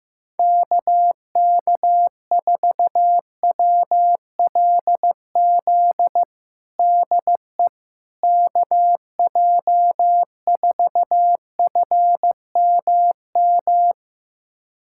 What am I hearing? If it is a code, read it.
Morse code, heard as a keyed tone, about 15 wpm, 700 Hz. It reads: KK4WLZ DE KJ4FMM
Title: USA Callsign DE Callsign - 6 Characters - 15wpm